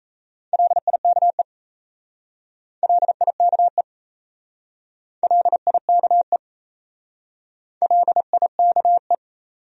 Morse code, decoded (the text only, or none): LIKE LIKE LIKE LIKE